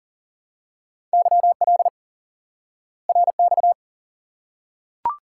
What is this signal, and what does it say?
Morse code, heard as a keyed tone, about 40 wpm, 700 Hz. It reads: YL RX
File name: IQE Course 3 - Lesson 89 - YAGI - Two Elements - 40wpm